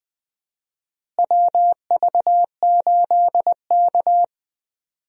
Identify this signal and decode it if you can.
Morse code, heard as a keyed tone, about 20 wpm, 700 Hz. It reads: WV8K
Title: USA Call Signs - 4 Characters - Repeated 3x - 20wpm